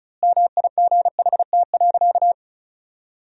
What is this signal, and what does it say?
Morse code, heard as a keyed tone, about 34 wpm, 700 Hz. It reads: MIGHT.
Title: 3x Word-Spacing - Sentences from Top 500 Words - 35wpm